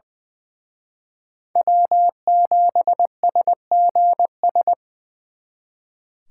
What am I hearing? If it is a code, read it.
Morse code, heard as a keyed tone, about 24 wpm, 700 Hz. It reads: W7SGS
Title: Speed Racing - USA Call Signs 20wpm